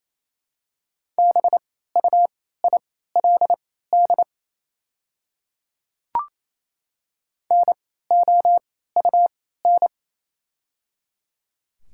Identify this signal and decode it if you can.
Morse code, heard as a keyed tone, about 28 wpm, spaced as for 16 wpm, 700 Hz. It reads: BUILD NOUN
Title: 3x Character Spacing - Top 500 Words - 28wpm